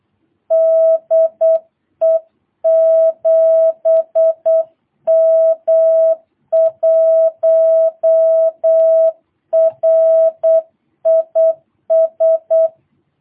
Morse code, heard as a keyed tone, about 8 wpm, 650 Hz. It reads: DE 7M1RIS
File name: jh8cbh / 2022cw